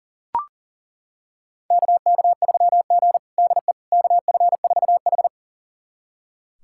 Morse code, heard as a keyed tone, about 40 wpm, 700 Hz. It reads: KK3G DE KF4H
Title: USA Callsign DE Callsign - 4 Characters - 40wpm